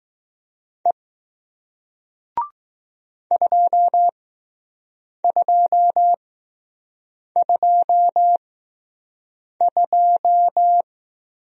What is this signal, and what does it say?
Morse code, heard as a keyed tone, about 20 wpm, spaced as for 6 wpm, 700 Hz. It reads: E 2222